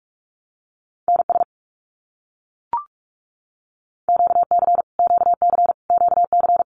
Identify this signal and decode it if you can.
Morse code, heard as a keyed tone, about 44 wpm, 700 Hz. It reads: NS QC QC QC
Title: Canadian Provinces and Territories - Repeated 3x - 45wpm